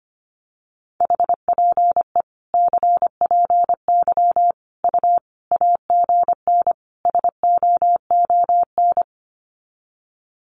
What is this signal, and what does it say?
Morse code, heard as a keyed tone, about 25 wpm, 700 Hz. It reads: HPE CPY U AGN SOON
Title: Common QSO Phrases 25wpm